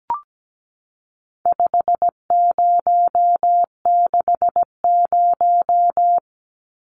Morse code, heard as a keyed tone, about 17 wpm, 700 Hz. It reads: 5060
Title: Numbers - 4-Digit 17wpm